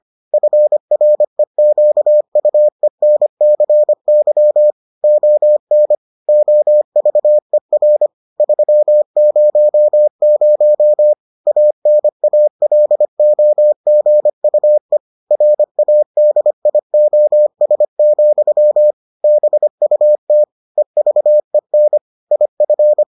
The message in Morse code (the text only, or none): FREQUENCY ON OVER 300 ANALOGUE RADIOS, BUT EVEN IF